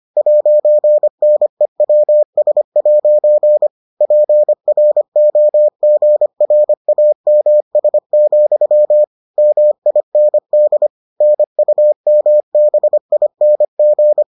Morse code, heard as a keyed tone, about 25 wpm, 600 Hz. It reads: 'NEWS' PROGRAMS, MIND NUMBING